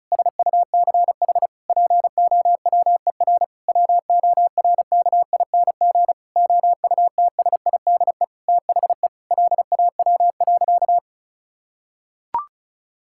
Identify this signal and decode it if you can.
Morse code, heard as a keyed tone, about 36 wpm, 700 Hz. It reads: SUCH POWER WORKING OUTSIDE THE LAW.
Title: Sentences from Top 1000 Words - Review Part 6 - 35wpm